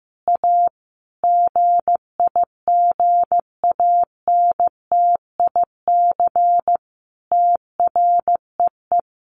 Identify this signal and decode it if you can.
Morse code, heard as a keyed tone, about 14 wpm, 700 Hz. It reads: A GIGANTIC TREE
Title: A Sound Of Thunder 15wpm